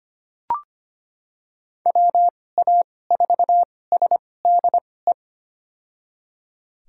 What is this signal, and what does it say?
Morse code, heard as a keyed tone, about 25 wpm, spaced as for 18 wpm, 700 Hz. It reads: WA4SDE
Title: USA Call Signs - 6 Characters - 2x Character Spacing - 25wpm